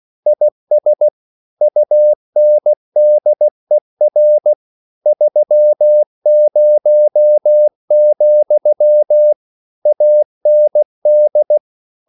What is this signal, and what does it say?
Morse code, heard as a keyed tone, about 16 wpm, 600 Hz. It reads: IS UNDER 30, AND
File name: CW / Practice / EnglishQuotations0216